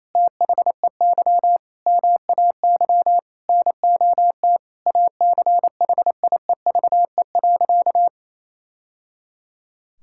Morse code, heard as a keyed tone, about 28 wpm, 700 Hz. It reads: THEY MAY NOT ACHIEVE.
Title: Sentences from Top 1000 Words - Review Part 6 - 28wpm